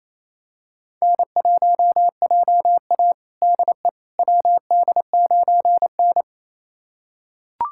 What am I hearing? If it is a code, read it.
Morse code, heard as a keyed tone, about 28 wpm, 700 Hz. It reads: N1JA DE WD9N
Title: USA Callsign DE Callsign - 4 Characters - 28wpm